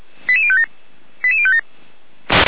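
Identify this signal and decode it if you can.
Five-tone selective call (ZVEI), heard as quick runs of five tones, tones of about 70 ms. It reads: 88057 88057
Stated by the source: ZVEI five-tone selective call 88106 received by SDR.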